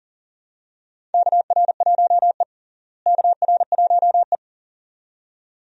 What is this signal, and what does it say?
Morse code, heard as a keyed tone, about 40 wpm, 700 Hz. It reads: KR1E KR1E
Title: USA Call Signs - 4 Characters - Repeated 2x - 40wpm